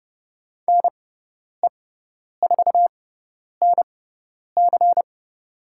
Morse code, heard as a keyed tone, about 30 wpm, spaced as for 10 wpm, 700 Hz. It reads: NE4NC
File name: Lesson 18g - Letter m - Call Signs 30wpm